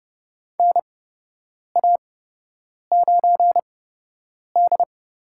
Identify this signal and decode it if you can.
Morse code, heard as a keyed tone, about 30 wpm, spaced as for 8 wpm, 700 Hz. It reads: NA9D
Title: USA Call Signs - 4 Characters - 8x Character Spacing - 30wpm